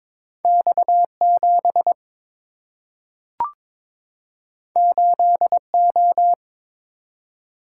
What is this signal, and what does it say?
Morse code, heard as a keyed tone, about 22 wpm, 700 Hz. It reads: X7 8O
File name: practice / 22wpm / Two Letter-Number Combinations - 22wpm